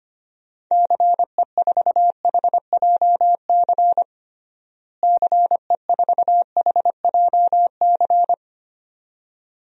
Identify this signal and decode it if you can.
Morse code, heard as a keyed tone, about 25 wpm, 700 Hz. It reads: CE4HJC CE4HJC
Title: DXCC-100 Call Signs - 6 Characters - Repeated 2x - 25wpm